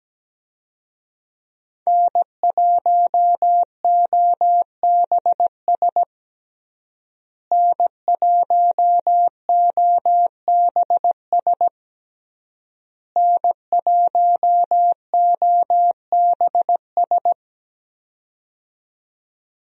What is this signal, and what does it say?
Morse code, heard as a keyed tone, about 17 wpm, 700 Hz. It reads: N1OBS N1OBS N1OBS
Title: USA Call Signs - 5 Characters - Repeated 3x - 17wpm